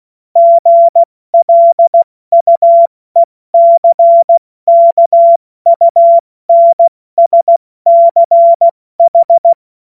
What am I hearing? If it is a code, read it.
Morse code, heard as a keyed tone, about 16 wpm, 680 Hz. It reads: GLUECKUNSCH
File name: Morsekurs / Witze / Witz 28. Dezember 2021, 70 F80